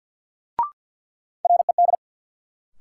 Morse code, heard as a keyed tone, about 47 wpm, 700 Hz. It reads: RED
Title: Sentences from Top 700 Words - New Word Drill - 50wpm